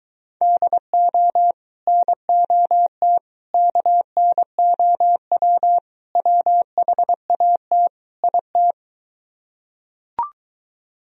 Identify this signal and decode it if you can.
Morse code, heard as a keyed tone, about 23 wpm, 700 Hz. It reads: DO NOT KNOW WHAT IT